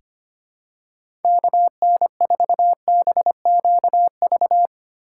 Morse code, heard as a keyed tone, about 25 wpm, 700 Hz. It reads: KN4BQV
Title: USA Call Signs - 6 Characters - 3x Character Spacing - 25wpm